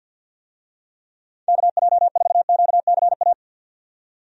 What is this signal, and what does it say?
Morse code, heard as a keyed tone, about 49 wpm, 700 Hz. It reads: KJ4QCA
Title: USA Call Signs - 6 Characters - 50wpm